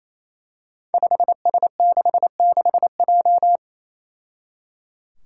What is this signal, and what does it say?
Morse code, heard as a keyed tone, about 28 wpm, 700 Hz. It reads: 5S66J
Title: Five Challenging Letter-Number Combinations 28wpm